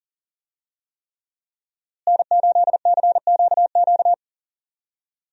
Morse code, heard as a keyed tone, about 40 wpm, 700 Hz. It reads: N8CQQ
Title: USA Call Signs - 5 Characters - Repeated 3x - 40wpm